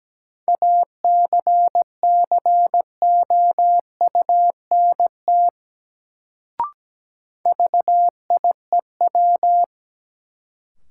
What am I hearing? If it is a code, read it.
Morse code, heard as a keyed tone, about 17 wpm, 700 Hz. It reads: ACCOUNT VIEW
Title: Sentences from Top 900 Words - New Word Drill - 17wpm